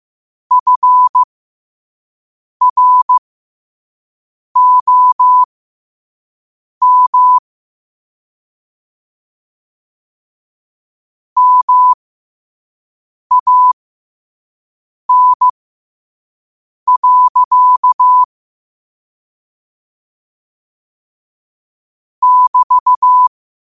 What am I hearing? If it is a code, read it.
Morse code, heard as a keyed tone, about 15 wpm, spaced as for 5 wpm, 1000 Hz. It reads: FROM MAN. <BT>
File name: qotdfiles / 5wpm